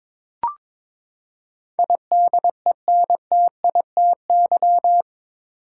Morse code, heard as a keyed tone, about 23 wpm, 700 Hz. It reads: IDENTITY
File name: practice / 22wpm / Eight Letter Words - 22wpm